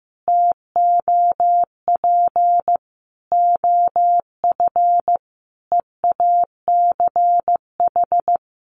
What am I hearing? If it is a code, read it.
Morse code, heard as a keyed tone, about 15 wpm, 700 Hz. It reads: TOP OF EACH
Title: Sentences from Top 700 Words 15wpm